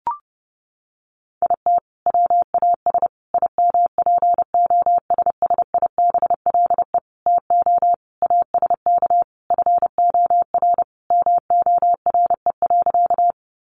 Morse code, heard as a keyed tone, about 30 wpm, 700 Hz. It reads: IT WAS IMPOSSIBLE TO ASK FOR MORE.
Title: Sentences from Top 1100 Words 30wpm